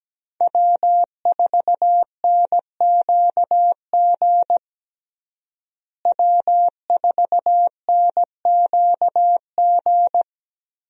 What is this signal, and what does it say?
Morse code, heard as a keyed tone, about 17 wpm, 700 Hz. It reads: W4NQG W4NQG
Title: USA Call Signs - 5 Characters - Repeated 2x - 17wpm